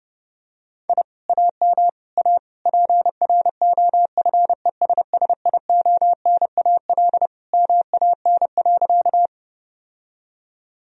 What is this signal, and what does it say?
Morse code, heard as a keyed tone, about 30 wpm, 700 Hz. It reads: I AM A PROFESSIONAL MAN.
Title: Sentences from Top 1000 Words - Review Part 4 - 30wpm